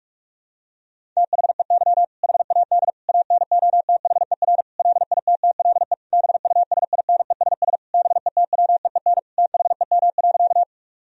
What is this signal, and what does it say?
Morse code, heard as a keyed tone, about 42 wpm, 700 Hz. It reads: THEY HAD ANOTHER LITTLE BUSINESS BETWEEN THEM.
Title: Sentences from Top 1000 Words - Review Part 1 - 45wpm